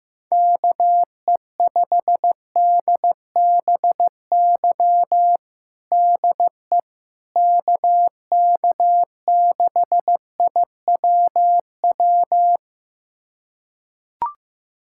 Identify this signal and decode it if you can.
Morse code, heard as a keyed tone, about 15 wpm, 700 Hz. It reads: KE5DBY DE KK6IWW
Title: USA Callsign DE Callsign - 6 Characters - 15wpm